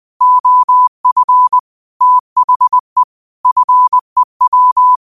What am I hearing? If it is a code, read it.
Morse code, heard as a keyed tone, about 20 wpm, 1000 Hz. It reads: OF THE FEW
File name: qotdfiles / 20wpm